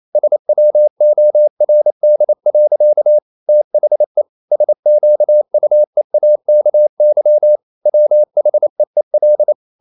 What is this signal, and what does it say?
Morse code, heard as a keyed tone, about 28 wpm, 600 Hz. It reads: SWORD. THE SQUEAKY WHEEL
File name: CW / Practice / EnglishSayings28